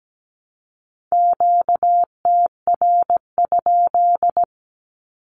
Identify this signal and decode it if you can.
Morse code, heard as a keyed tone, about 17 wpm, 700 Hz. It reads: QTR?
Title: Q-Codes Repeated 3x 17wpm